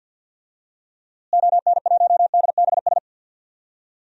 Morse code, heard as a keyed tone, about 52 wpm, 700 Hz. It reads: ON1DBS
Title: DXCC-100 Call Signs - 6 Characters - Repeated 2x - 50wpm